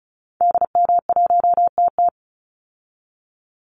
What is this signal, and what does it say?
Morse code, heard as a keyed tone, about 34 wpm, 700 Hz. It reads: DM1TT
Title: Call Signs Repeated 3x - 35wpm